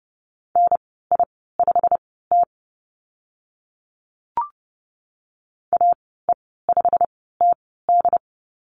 Morse code, heard as a keyed tone, about 30 wpm, spaced as for 17 wpm, 700 Hz. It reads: NI5T AE5TD
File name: Lesson 15g - Number 5 - Call Signs 30wpm